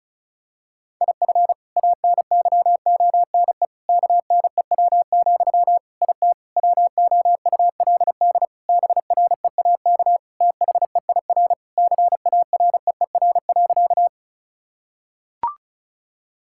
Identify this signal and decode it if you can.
Morse code, heard as a keyed tone, about 36 wpm, 700 Hz. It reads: IF ANYONE KNEW, IT WOULD BREAK THEIR CAREER.
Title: Sentences from Top 1000 Words - Review Part 6 - 35wpm